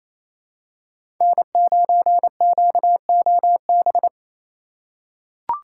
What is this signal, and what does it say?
Morse code, heard as a keyed tone, about 28 wpm, 700 Hz. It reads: N9QOB
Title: USA Call Signs - 5 Characters - 6x Character Spacing - 28wpm